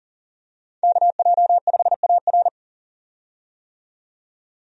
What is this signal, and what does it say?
Morse code, heard as a keyed tone, about 40 wpm, 700 Hz. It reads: KJ5AR
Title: USA Call Signs - 5 Characters - 40wpm